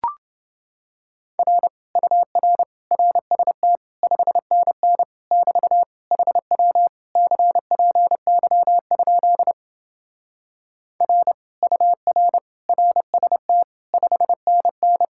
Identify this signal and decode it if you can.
Morse code, heard as a keyed tone, about 28 wpm, 700 Hz. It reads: R UR RST 5NN <BT> HW CPY? R UR RST 5NN